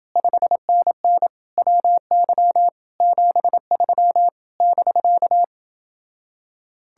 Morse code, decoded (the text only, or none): 5NN WY 73 <BK>